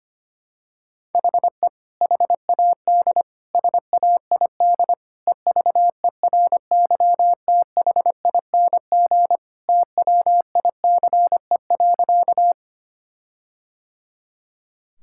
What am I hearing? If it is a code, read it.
Morse code, heard as a keyed tone, about 25 wpm, 700 Hz. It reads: HE HAD SAID EVERYTHING TWICE.
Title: Sentences from Top 1600 Words 25wpm